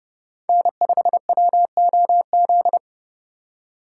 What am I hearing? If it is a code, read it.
Morse code, heard as a keyed tone, about 28 wpm, 700 Hz. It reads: N5WOZ